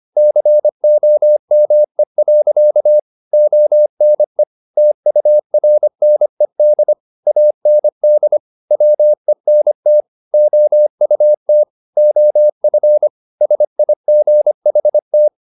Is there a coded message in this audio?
Yes: Morse code from 0.2 to 15.3 s, heard as a keyed tone, about 25 wpm, 600 Hz. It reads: COME. ONE TURNED AND WENT OUT OF SIGHT